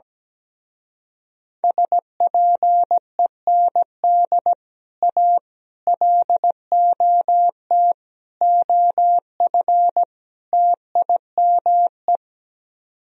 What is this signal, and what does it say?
Morse code, heard as a keyed tone, about 17 wpm, 700 Hz. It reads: SPEND A LOT OF TIME
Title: Speed-Racing - Sets of 5 Words 17wpm